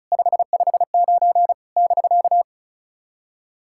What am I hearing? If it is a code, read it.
Morse code, heard as a keyed tone, about 36 wpm, 690 Hz. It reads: 559 <BK>
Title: SOTA QSO Practice 35wpm